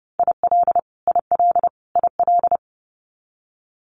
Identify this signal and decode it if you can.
Morse code, heard as a keyed tone, about 30 wpm, 700 Hz. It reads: IL IL IL
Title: US States Repeated 3x 30wpm